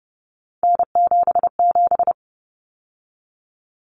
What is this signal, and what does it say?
Morse code, heard as a keyed tone, about 30 wpm, 700 Hz. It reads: N77
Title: Common Cut Numbers - 3-Digit 30wpm